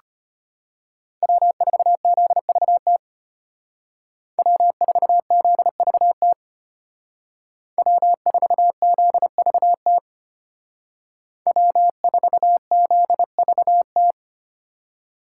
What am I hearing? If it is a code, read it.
Morse code, heard as a keyed tone, about 32 wpm, 700 Hz. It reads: W4ZVT W4ZVT W4ZVT W4ZVT